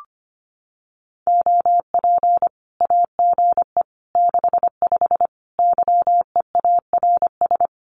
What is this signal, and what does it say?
Morse code, heard as a keyed tone, about 25 wpm, 700 Hz. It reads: OP AGE 65 YEARS
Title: Common QSO Phrases 25wpm